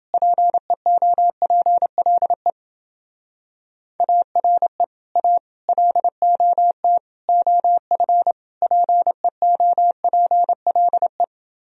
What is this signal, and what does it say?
Morse code, heard as a keyed tone, about 28 wpm, 700 Hz. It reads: PEOPLE ARE A LOT OF PEOPLE